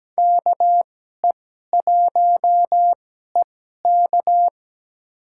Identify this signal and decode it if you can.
Morse code, heard as a keyed tone, about 17 wpm, spaced as for 12 wpm, 700 Hz. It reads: KE1EK
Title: USA Call Signs - 5 Characters - 2x Character Spacing - 17wpm